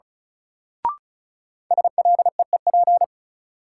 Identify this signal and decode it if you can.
Morse code, heard as a keyed tone, about 36 wpm, 700 Hz. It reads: SLEEP
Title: Sentences from Top 500 Words - New Word Drill - 35wpm